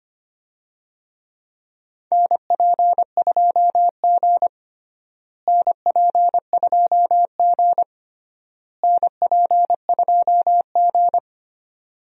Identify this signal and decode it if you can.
Morse code, heard as a keyed tone, about 25 wpm, 700 Hz. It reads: NP2G NP2G NP2G
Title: USA Call Signs - 4 Characters - Repeated 3x - 25wpm